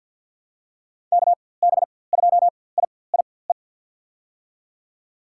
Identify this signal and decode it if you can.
Morse code, heard as a keyed tone, about 50 wpm, spaced as for 23 wpm, 700 Hz. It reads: KB2IIE
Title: USA Call Signs - 6 Characters - 4x Character Spacing - 50wpm